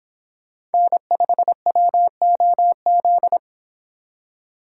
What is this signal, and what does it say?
Morse code, heard as a keyed tone, about 30 wpm, 700 Hz. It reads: N5WOZ